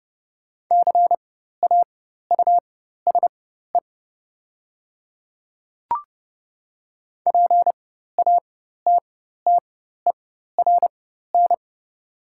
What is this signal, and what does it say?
Morse code, heard as a keyed tone, about 30 wpm, spaced as for 14 wpm, 700 Hz. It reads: CAUSE PATTERN
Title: 4x Character Spacing - Top 500 Words - 30wpm